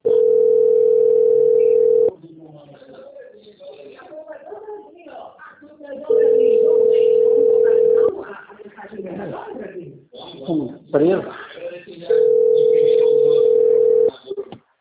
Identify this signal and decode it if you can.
Ringback tone (North America), 440 + 480 Hz, with speech.
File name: out- - - - - .19660